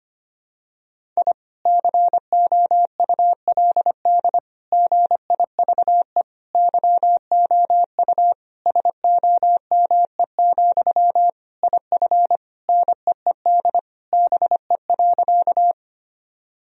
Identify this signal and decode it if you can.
Morse code, heard as a keyed tone, about 25 wpm, 700 Hz. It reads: I COULD GIVE YOU SOME, IF NEED BE.
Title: Sentences from Top 1000 Words - Review Part 1 - 25wpm